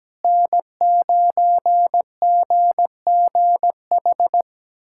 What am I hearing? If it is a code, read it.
Morse code, heard as a keyed tone, about 17 wpm, 700 Hz. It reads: N9GGH
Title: USA Callsign DE Callsign 17wpm